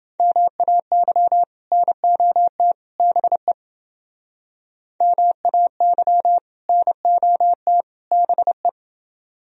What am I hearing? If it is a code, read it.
Morse code, heard as a keyed tone, about 28 wpm, 700 Hz. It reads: MAY NOT BE MAY NOT BE